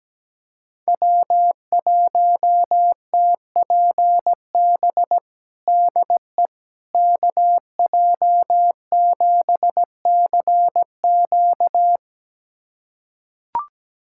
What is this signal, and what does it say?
Morse code, heard as a keyed tone, about 17 wpm, 700 Hz. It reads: W1TPB DE KJ7CQ
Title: USA Callsign DE Callsign - 5 Characters - 17wpm